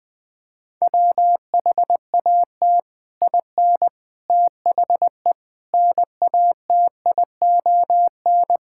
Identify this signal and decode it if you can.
Morse code, heard as a keyed tone, about 20 wpm, 700 Hz. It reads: WHAT IN THE NATION
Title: Sentences from Top 1000 Words - Review Part 6 - 20wpm